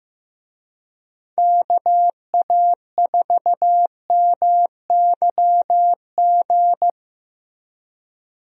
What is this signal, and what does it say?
Morse code, heard as a keyed tone, about 15 wpm, 700 Hz. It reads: KA4MYG
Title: USA Call Signs - 6 Characters - Repeated 3x - 15wpm